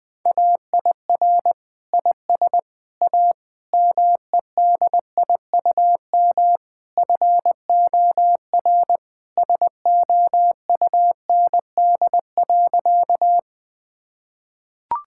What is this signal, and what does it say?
Morse code, heard as a keyed tone, about 20 wpm, 700 Hz. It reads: AIR IS A MEDIUM FOR SOUND.
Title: Sentences from Top 1000 Words - Review Part 6 - 20wpm